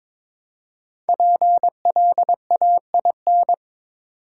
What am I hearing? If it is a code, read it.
Morse code, heard as a keyed tone, about 22 wpm, 700 Hz. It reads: PLAIN
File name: 5 Letter Words - First 3 Letters Given - 22wpm